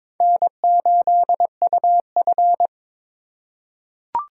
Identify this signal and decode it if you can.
Morse code, heard as a keyed tone, about 21 wpm, 700 Hz. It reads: N8UF
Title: USA Call Signs - 3x Character Spacing - 22wpm